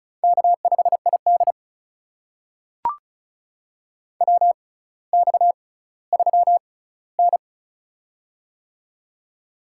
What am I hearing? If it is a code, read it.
Morse code, heard as a keyed tone, about 36 wpm, spaced as for 18 wpm, 700 Hz. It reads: K5ID WX3N
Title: USA Call Signs - 4 Characters - 6x Character Spacing - 35wpm